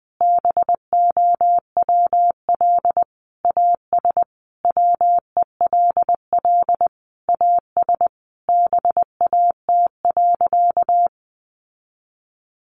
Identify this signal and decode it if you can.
Morse code, heard as a keyed tone, about 20 wpm, 700 Hz. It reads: BOWL AS WELL AS BAT.
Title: Sentences from Top 1500 Words 20wpm